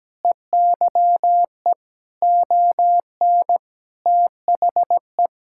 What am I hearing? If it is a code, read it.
Morse code, heard as a keyed tone, about 17 wpm, 700 Hz. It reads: EYE ON THE
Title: Sentences from Top 1000 Words - Review Part 5 - 17wpm